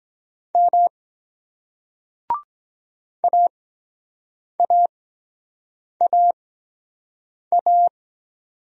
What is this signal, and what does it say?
Morse code, heard as a keyed tone, about 23 wpm, spaced as for 7 wpm, 700 Hz. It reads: M AAAA